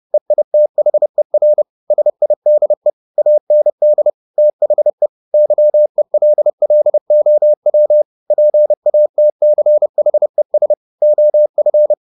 Morse code, heard as a keyed tone, about 30 wpm, 600 Hz. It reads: EITHER SIDE AND THE YELLOW PATCHES OF